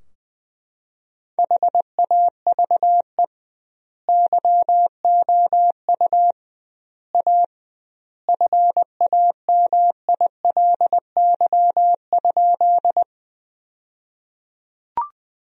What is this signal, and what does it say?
Morse code, heard as a keyed tone, about 20 wpm, 700 Hz. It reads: HAVE YOU A FAMILY?
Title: 2x Word-Spacing - Sentences from Top 400 Words - 20wpm